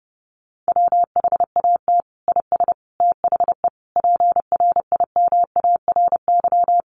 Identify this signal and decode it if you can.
Morse code, heard as a keyed tone, about 30 wpm, 700 Hz. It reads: WHAT IS THE PRIMARY
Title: Sentences from Top 1100 Words 30wpm